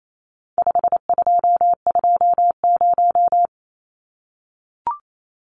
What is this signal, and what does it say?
Morse code, heard as a keyed tone, about 28 wpm, 700 Hz. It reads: 5220
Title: Numbers - 4-Digit 28wpm